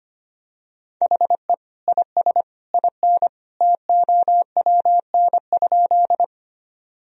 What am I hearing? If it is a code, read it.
Morse code, heard as a keyed tone, about 25 wpm, 700 Hz. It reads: HE IS IN TOWN?
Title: Sentences from Top 1000 Words - Review Part 2 - 25wpm